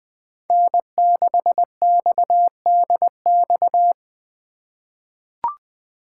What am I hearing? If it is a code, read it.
Morse code, heard as a keyed tone, about 21 wpm, 700 Hz. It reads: N6XDX
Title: USA Call Signs - 2x Character Spacing - 20wpm